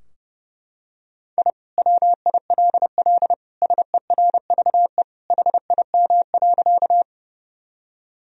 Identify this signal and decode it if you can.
Morse code, heard as a keyed tone, about 32 wpm, 700 Hz. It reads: I WILL SERVE HIM.
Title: Sentences from Top 1000 Words - Review Part 2 - 30wpm